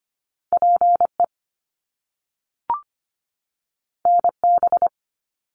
Morse code, heard as a keyed tone, about 25 wpm, 700 Hz. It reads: PE NB
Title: Canadian Provinces and Territories 25wpm